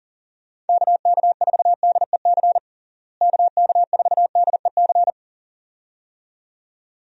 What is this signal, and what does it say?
Morse code, heard as a keyed tone, about 40 wpm, 700 Hz. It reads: KK4DEC KK4DEC
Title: USA Call Signs - 6 Characters - Repeated 2x - 40wpm